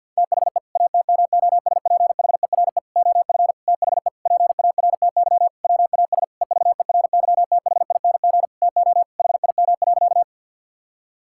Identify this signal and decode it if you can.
Morse code, heard as a keyed tone, about 50 wpm, 700 Hz. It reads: THE ATMOSPHERE OF THE PARTY WAS EVERYTHING TO HIM.
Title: Sentences from Top 1600 Words 50wpm